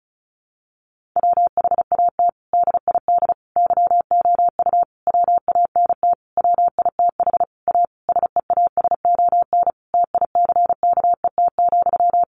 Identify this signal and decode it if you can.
Morse code, heard as a keyed tone, about 34 wpm, 700 Hz. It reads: WHAT DID YOU WANT WITH A SEASON TICKET,
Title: Sentences from Top 1400 Words 35wpm